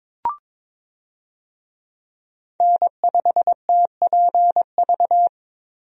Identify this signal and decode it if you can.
Morse code, heard as a keyed tone, about 22 wpm, 700 Hz. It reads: N5TPV
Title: USA Call Signs - 5 Characters - Repeated 2x - 22wpm